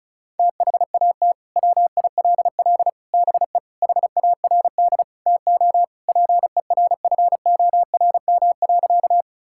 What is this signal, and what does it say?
Morse code, heard as a keyed tone, about 36 wpm, 700 Hz. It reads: THAT WILL BE HARD TO PERFORM.
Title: Sentences from Top 1000 Words - Review Part 5 - 35wpm